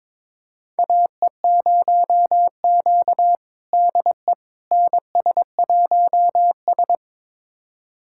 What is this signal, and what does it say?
Morse code, heard as a keyed tone, about 22 wpm, 700 Hz. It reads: AE0Q DE NS1S
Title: USA Callsign DE Callsign - 4 Characters - 22wpm